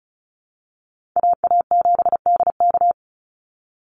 Morse code, heard as a keyed tone, about 34 wpm, 700 Hz. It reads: AA7DK
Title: NA SOTA Chaser Callsigns 35wpm